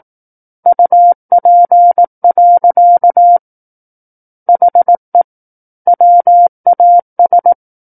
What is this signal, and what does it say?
Morse code, heard as a keyed tone, about 18 wpm, 700 Hz. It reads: UP. HE WAS
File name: Book One / 09-Chapter 9 - 18 WPM